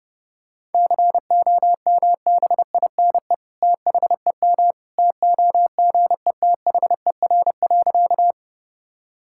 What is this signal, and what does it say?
Morse code, heard as a keyed tone, about 32 wpm, 700 Hz. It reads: COMBINE THEM TOGETHER.
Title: Sentences from Top 1600 Words 30wpm